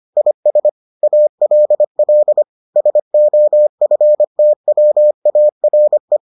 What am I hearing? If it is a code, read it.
Morse code, heard as a keyed tone, about 25 wpm, 600 Hz. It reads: IS ALL SOFTWARE